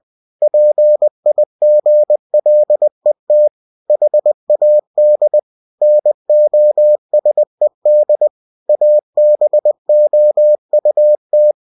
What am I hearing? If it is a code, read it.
Morse code, heard as a keyed tone, about 20 wpm, 600 Hz. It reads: PIGLET HAD NOSED ABOUT
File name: cw-books / winnie-the-pooh / winnie-20wpm-page-0091